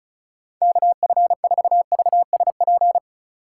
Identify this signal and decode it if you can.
Morse code, heard as a keyed tone, about 39 wpm, 700 Hz. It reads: KF4VSP